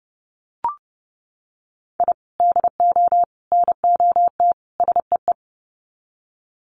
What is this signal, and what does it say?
Morse code, heard as a keyed tone, about 30 wpm, 700 Hz. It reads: I DO NOT SEE
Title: Sets of 4 Words 30wpm